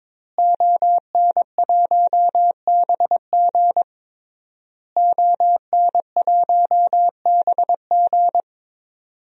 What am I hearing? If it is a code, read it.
Morse code, heard as a keyed tone, about 23 wpm, 700 Hz. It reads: ON1BG ON1BG
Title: DXCC-100 Call Signs - 5 Characters - Repeated 3x - 22wpm